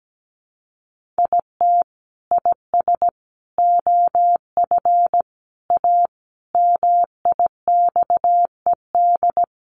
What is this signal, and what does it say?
Morse code, heard as a keyed tone, about 17 wpm, 700 Hz. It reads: IT IS OF A MIXED
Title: Sentences from Top 1500 Words 17wpm